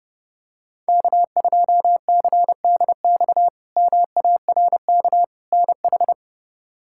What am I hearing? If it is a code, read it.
Morse code, heard as a keyed tone, about 32 wpm, 700 Hz. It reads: K2CDX MARK NH
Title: K1USN SST QSO Practice - 30wpm